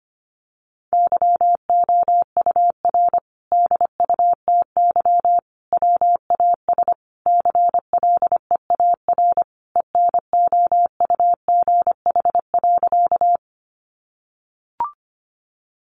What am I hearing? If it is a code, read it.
Morse code, heard as a keyed tone, about 25 wpm, 700 Hz. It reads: YOUR DUTY WAS CLEAR ENOUGH.
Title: Sentences from Top 1500 Words 25wpm